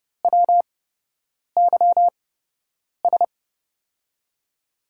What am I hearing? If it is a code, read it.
Morse code, heard as a keyed tone, about 30 wpm, spaced as for 8 wpm, 700 Hz. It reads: WYS
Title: USA Call Signs - 8x Character Spacing - 30wpm